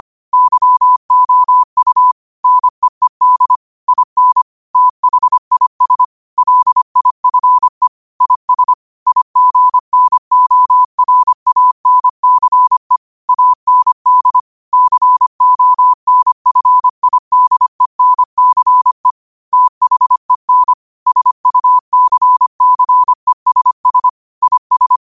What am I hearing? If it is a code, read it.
Morse code, heard as a keyed tone, about 25 wpm, 1000 Hz. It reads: YOU NEED IN THIS LIFE IS IGNORANCE AND CONFIDENCE THEN SUCCESS IS
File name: qotdfiles / 25wpm